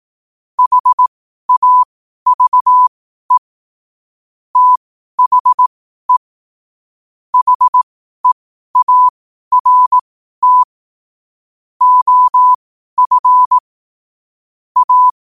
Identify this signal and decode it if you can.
Morse code, heard as a keyed tone, about 18 wpm, spaced as for 12 wpm, 1000 Hz. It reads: HAVE THE HEART OF A
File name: qotdfiles / 12wpm